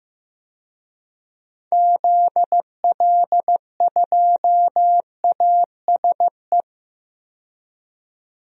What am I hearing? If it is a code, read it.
Morse code, heard as a keyed tone, about 15 wpm, 700 Hz. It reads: ZL2ASE
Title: DXCC-100 Call Signs - 6 Characters - 15wpm